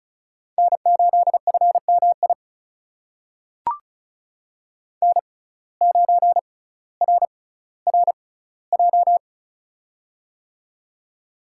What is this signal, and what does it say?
Morse code, heard as a keyed tone, about 36 wpm, spaced as for 18 wpm, 700 Hz. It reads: N8FMI N9RRJ
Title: USA Call Signs - 5 Characters - 6x Character Spacing - 35wpm